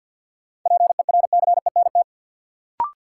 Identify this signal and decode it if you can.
Morse code, heard as a keyed tone, about 53 wpm, 700 Hz. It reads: PERCENT
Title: Sets of 2 Words - New Word Drill - 50wpm